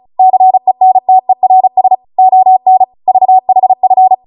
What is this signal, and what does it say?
Morse code, heard as a keyed tone, about 36 wpm, 750 Hz. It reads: CENTERS ON VHF